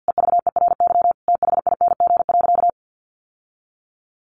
Morse code, heard as a keyed tone, about 49 wpm, 700 Hz. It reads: EVERYTHING.
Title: Sentences from Top 1100 Words 50wpm